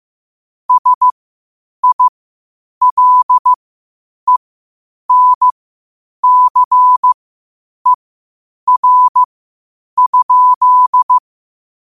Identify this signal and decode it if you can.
Morse code, heard as a keyed tone, about 15 wpm, spaced as for 8 wpm, 1000 Hz. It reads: SILENCER?
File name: qotdfiles / 8wpm